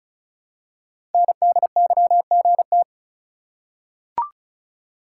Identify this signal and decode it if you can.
Morse code, heard as a keyed tone, about 36 wpm, 700 Hz. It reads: NDYGT
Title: Code Groups - 5 Letter:Numbers - 35wpm